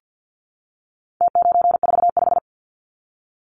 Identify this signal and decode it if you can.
Morse code, heard as a keyed tone, about 50 wpm, 700 Hz. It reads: T945
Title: Numbers - Contest Style - 4-Digit 50wpm